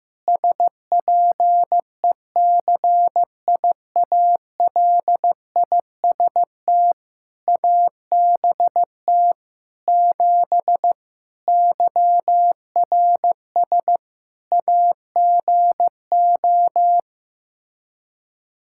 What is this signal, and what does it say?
Morse code, heard as a keyed tone, about 15 wpm, 700 Hz. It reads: SPECIALIST ABT 7 YRS AGO
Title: Practice QSO - Exchange 3 - More Info - 15wpm